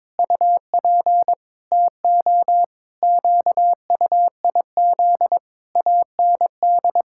Morse code, heard as a keyed tone, about 22 wpm, 700 Hz. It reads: UP TO QUIZ AND
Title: Pangram Challenge v2 - 22wpm